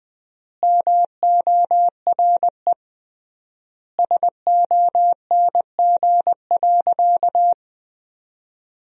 Morse code, heard as a keyed tone, about 20 wpm, 700 Hz. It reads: MORE SONG.
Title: 3x Word-Spacing - Sentences from Top 400 Words - 20wpm